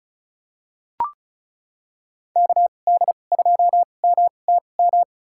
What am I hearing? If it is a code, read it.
Morse code, heard as a keyed tone, about 36 wpm, spaced as for 25 wpm, 700 Hz. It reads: KD2MTM
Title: USA Call Signs - 2x Character Spacing - 35wpm